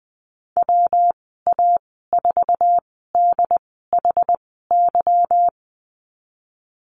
Morse code, heard as a keyed tone, about 20 wpm, spaced as for 14 wpm, 700 Hz. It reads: WA4DHY
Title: 167 Lesson 27g - Number 7 - Call Signs 20wpm